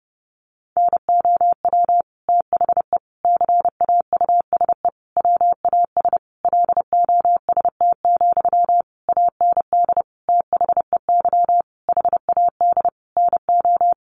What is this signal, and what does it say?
Morse code, heard as a keyed tone, about 30 wpm, 700 Hz. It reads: NOW THE CAUSE WAS LOST, AND THEY HAD NO
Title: Sentences from Top 1500 Words 30wpm